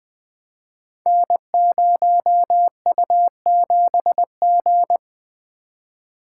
Morse code, heard as a keyed tone, about 20 wpm, 700 Hz. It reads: N0U7G
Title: Code Groups - 5 Letter:Numbers - 20wpm